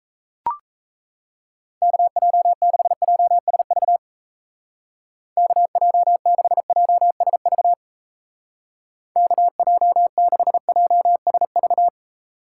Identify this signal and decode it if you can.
Morse code, heard as a keyed tone, about 34 wpm, 700 Hz. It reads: KJ6JSV KJ6JSV KJ6JSV
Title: Speed Racing - USA 2x3 Call Signs 28wpm